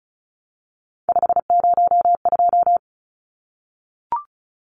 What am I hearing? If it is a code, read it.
Morse code, heard as a keyed tone, about 34 wpm, 700 Hz. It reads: 502
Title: Numbers - 3-Digit 35wpm